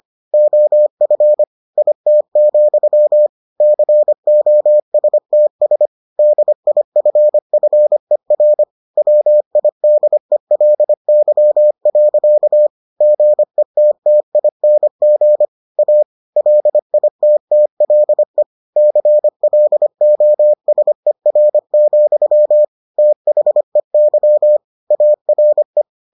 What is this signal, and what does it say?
Morse code, heard as a keyed tone, about 25 wpm, 600 Hz. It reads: OF IT, COSTS DIFFER WIDELY. GETTING A LITTLE CLOSER, THEY ARE